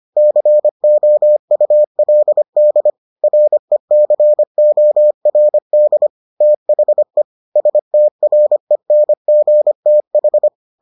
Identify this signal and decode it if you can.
Morse code, heard as a keyed tone, about 25 wpm, 600 Hz. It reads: COULD RECORD THE STRENGTH